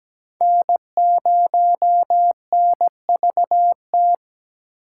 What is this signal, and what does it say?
Morse code, heard as a keyed tone, about 17 wpm, 700 Hz. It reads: N0NVT
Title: USA Call Signs - 5 Characters - 4x Character Spacing - 17wpm